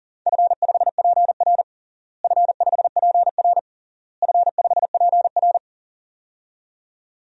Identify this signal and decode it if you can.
Morse code, heard as a keyed tone, about 40 wpm, 700 Hz. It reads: F5PR F5PR F5PR
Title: DXCC-100 Call Signs - 4 Characters - Repeated 3x - 40wpm